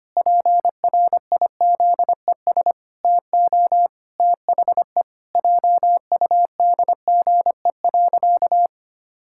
Morse code, heard as a keyed tone, about 25 wpm, 700 Hz. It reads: PRIZES TO THE JUDGE.
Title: Pangram Challenge v2 - 25wpm